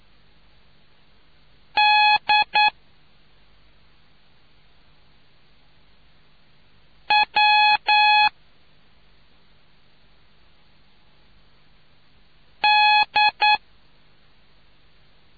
Morse code, heard as a keyed tone, about 9 wpm, 840 Hz. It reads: D W D